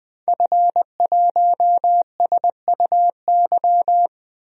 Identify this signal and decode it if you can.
Morse code, heard as a keyed tone, about 21 wpm, 700 Hz. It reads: F1SUY
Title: DXCC-100 Call Signs - 5 Characters - Repeated 2x - 20wpm